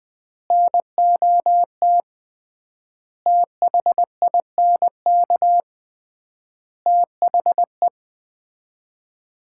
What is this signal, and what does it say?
Morse code, heard as a keyed tone, about 21 wpm, 700 Hz. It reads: NOT THINK THE
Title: 3x Word-Spacing - Sets of 5 Words - 20wpm